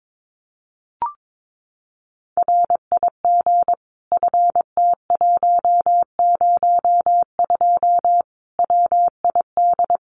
Morse code, heard as a keyed tone, about 22 wpm, 700 Hz. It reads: RIG FT102 WID
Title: Common QSO Phrases 22wpm